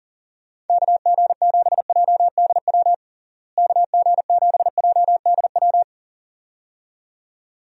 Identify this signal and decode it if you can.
Morse code, heard as a keyed tone, about 40 wpm, 700 Hz. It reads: KG7JDW KG7JDW
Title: USA Call Signs - 6 Characters - Repeated 2x - 40wpm